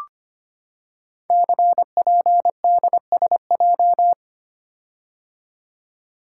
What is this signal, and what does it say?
Morse code, heard as a keyed tone, about 25 wpm, 700 Hz. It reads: CPDSJ
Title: Code Groups - 5 Letter:Numbers - 25wpm